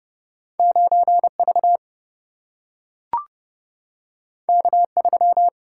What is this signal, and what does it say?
Morse code, heard as a keyed tone, about 30 wpm, 700 Hz. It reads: 9 V K 3
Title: Single-Letter Single-Number 30wpm